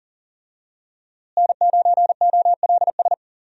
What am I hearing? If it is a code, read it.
Morse code, heard as a keyed tone, about 40 wpm, 700 Hz. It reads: N9OLS
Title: USA Call Signs - 5 Characters - Repeated 2x - 40wpm